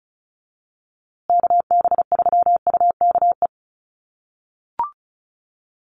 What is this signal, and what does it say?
Morse code, heard as a keyed tone, about 34 wpm, 700 Hz. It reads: KB3UKE
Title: USA Call Signs Repeated 2x 35wpm